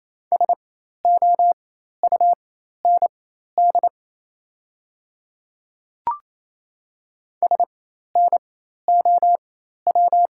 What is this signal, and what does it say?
Morse code, heard as a keyed tone, about 28 wpm, spaced as for 13 wpm, 700 Hz. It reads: SOUND SNOW
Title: 4x Character Spacing - Top 500 Words - 28wpm